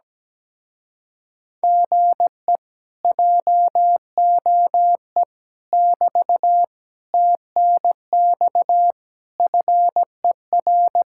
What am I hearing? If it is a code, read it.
Morse code, heard as a keyed tone, about 17 wpm, 700 Hz. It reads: GE JOE <BT> TNX FER
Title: Speed-Racing Common QSO Phrases - 17wpm